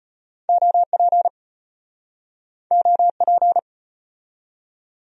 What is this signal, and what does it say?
Morse code, heard as a keyed tone, about 31 wpm, 700 Hz. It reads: OP OP
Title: Speed-Racing Abbreviations - 25wpm